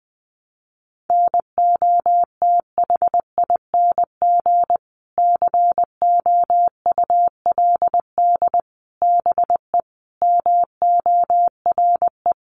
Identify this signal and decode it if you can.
Morse code, heard as a keyed tone, about 20 wpm, 700 Hz. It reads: NOTHING COULD BE MORE
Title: Sentences from Top 500 Words 20wpm